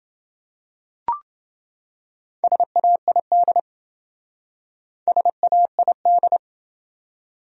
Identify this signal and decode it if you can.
Morse code, heard as a keyed tone, about 28 wpm, 700 Hz. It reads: SAID SAID